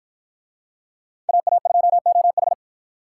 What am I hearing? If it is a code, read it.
Morse code, heard as a keyed tone, about 53 wpm, 700 Hz. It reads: AA2OH